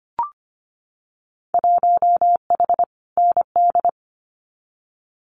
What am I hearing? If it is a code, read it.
Morse code, heard as a keyed tone, about 25 wpm, 700 Hz. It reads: 1H ND
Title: Winter Field Day Exchanges 25wpm